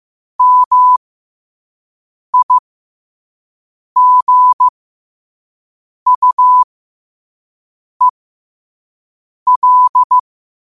Morse code, heard as a keyed tone, about 15 wpm, spaced as for 5 wpm, 1000 Hz. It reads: MIGUEL
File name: qotdfiles / 5wpm